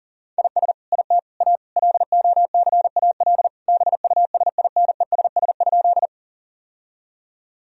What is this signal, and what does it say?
Morse code, heard as a keyed tone, about 40 wpm, 700 Hz. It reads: IS IT A LOCAL BUSINESS?
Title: Sentences from Top 1000 Words - Review Part 3 - 40wpm